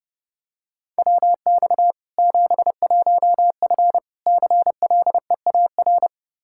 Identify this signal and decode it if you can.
Morse code, heard as a keyed tone, about 30 wpm, 700 Hz. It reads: WX 71F CLEAR
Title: K1USN SST QSO Practice 30wpm